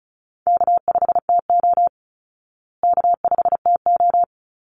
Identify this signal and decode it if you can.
Morse code, heard as a keyed tone, about 34 wpm, 700 Hz. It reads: K5TO K5TO
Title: Easy Call Signs Repeated 2x - 35wpm